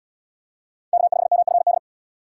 Morse code, heard as a keyed tone, about 77 wpm, 700 Hz. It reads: K5MFD